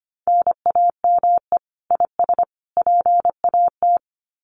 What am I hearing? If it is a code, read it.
Morse code, heard as a keyed tone, about 24 wpm, 700 Hz. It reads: NAME IS PAT
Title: Common QSO Phrases 25wpm